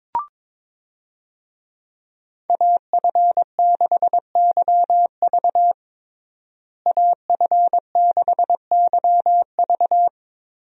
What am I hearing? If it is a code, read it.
Morse code, heard as a keyed tone, about 22 wpm, 700 Hz. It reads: AF6YV AF6YV
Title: USA Call Signs - 5 Characters - Repeated 2x - 22wpm